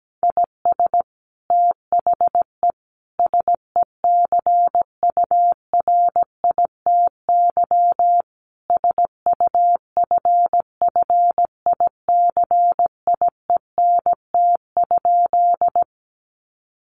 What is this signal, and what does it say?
Morse code, heard as a keyed tone, about 17 wpm, 700 Hz. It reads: IS THE SECURITY SUFFICIENT?
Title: Sentences from Top 1400 Words 17wpm